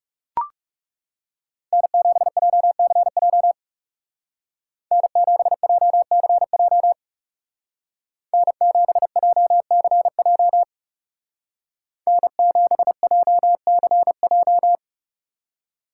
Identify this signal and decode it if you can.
Morse code, heard as a keyed tone, about 37 wpm, 700 Hz. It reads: N7JCJ N7JCJ N7JCJ N7JCJ
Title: Speed Racing - USA 1x3 Call Signs 30wpm